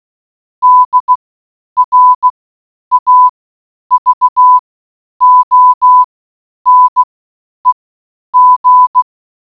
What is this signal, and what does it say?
Morse code, heard as a keyed tone, about 16 wpm, spaced as for 9 wpm, 1000 Hz. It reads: DRAVONEG
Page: Rare experiment morse code encode inside the watermelon.